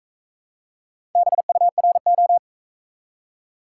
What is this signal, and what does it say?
Morse code, heard as a keyed tone, about 40 wpm, 700 Hz. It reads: BURO